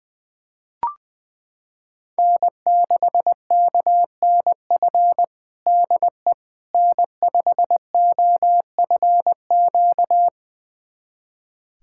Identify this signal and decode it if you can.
Morse code, heard as a keyed tone, about 20 wpm, 700 Hz. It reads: N6KNF DE N5OFQ
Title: USA Callsign DE Callsign - 5 Characters - 20wpm